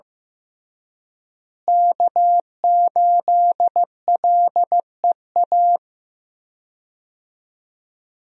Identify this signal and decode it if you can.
Morse code, heard as a keyed tone, about 15 wpm, 700 Hz. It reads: K8LEA
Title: USA Call Signs - 5 Characters - Repeated 3x - 15wpm